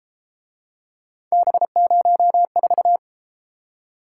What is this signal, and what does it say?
Morse code, heard as a keyed tone, about 32 wpm, 700 Hz. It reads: B04